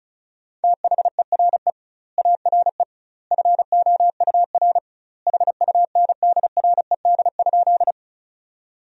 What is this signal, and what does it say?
Morse code, heard as a keyed tone, about 35 wpm, 700 Hz. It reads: THERE ARE FOUR HUNDRED?
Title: 2x Word-Spacing - Sentences from Top 400 Words - 35wpm